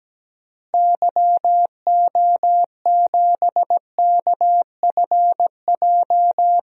Morse code, heard as a keyed tone, about 17 wpm, 700 Hz. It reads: YO7KFJ
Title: DXCC-100 Call Signs - 6 Characters - Repeated 2x - 17wpm